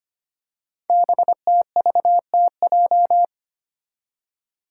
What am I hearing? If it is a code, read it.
Morse code, heard as a keyed tone, about 25 wpm, 700 Hz. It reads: BTVTJ
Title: Code Groups - 5 Letters - 25wpm